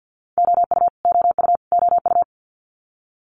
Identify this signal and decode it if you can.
Morse code, heard as a keyed tone, about 50 wpm, 690 Hz. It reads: OU OU OU
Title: Multi-letter Phonemes Repeated 3x 50wpm